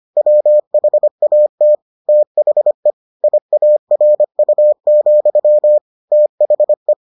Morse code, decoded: WHAT THE IARU, THE

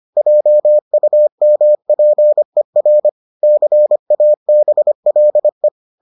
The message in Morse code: JUMPER CABLE

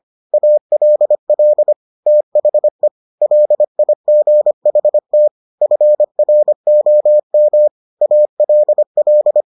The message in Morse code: ALL THE LIGHT FROM ALL

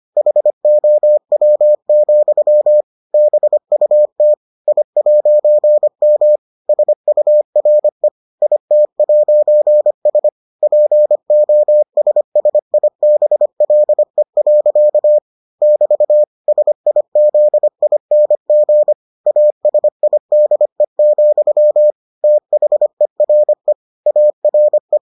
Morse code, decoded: HOW, BUT I'M SURE IT'S POSSIBLE. <BT> SIZING ASIDE, THERE ARE

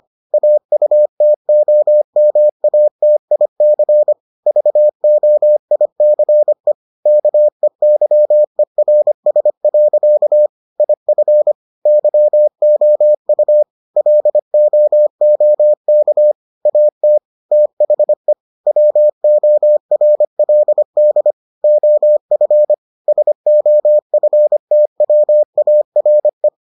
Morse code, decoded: AUTOMATIC VOICE KEYERS. IF YOU LOOK AT THE WORLD OF SOFTWARE

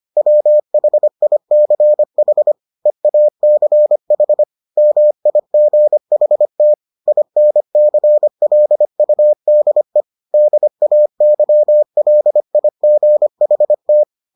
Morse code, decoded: WHICH EACH MIGHT INCLUDE DAYLIGHT

